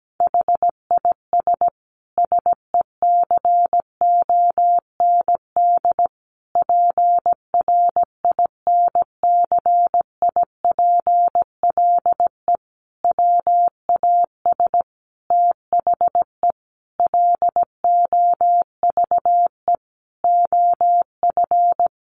Morse code, heard as a keyed tone, about 17 wpm, 700 Hz. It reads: HIS SECOND PRINCIPLE WAS THE LOVE OF
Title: Sentences from Top 1400 Words 17wpm